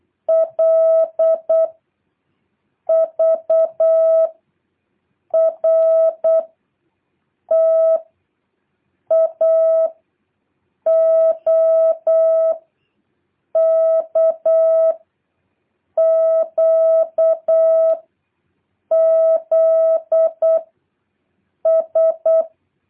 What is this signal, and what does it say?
Morse code, heard as a keyed tone, about 8 wpm, 650 Hz. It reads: L V R T A O K Q Z S
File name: jh8cbh / 2023cw